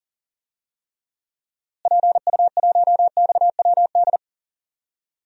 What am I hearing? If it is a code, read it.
Morse code, heard as a keyed tone, about 40 wpm, 700 Hz. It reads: PU1XWD
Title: DXCC-100 Call Signs - 6 Characters - Repeated 2x - 40wpm